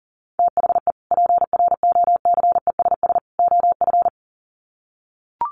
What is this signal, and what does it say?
Morse code, heard as a keyed tone, about 40 wpm, 700 Hz. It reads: THE PROCESS OF
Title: Sets of 3 Words 40wpm